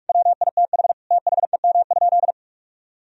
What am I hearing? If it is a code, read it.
Morse code, heard as a keyed tone, about 46 wpm, 700 Hz. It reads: WITH THEM?
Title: Sentences from Top 1000 Words - Review Part 3 - 45wpm